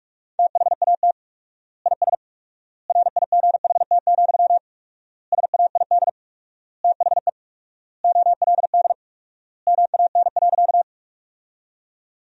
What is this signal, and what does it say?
Morse code, heard as a keyed tone, about 46 wpm, 700 Hz. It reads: THAT IS RIGHT, SAID THE OLD MAN.
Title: 4x Word-Spacing - Sentences from Top 200 Words - 45wpm